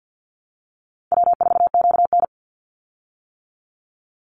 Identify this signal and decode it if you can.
Morse code, heard as a keyed tone, about 50 wpm, 700 Hz. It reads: W4QN
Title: Call Signs - 50wpm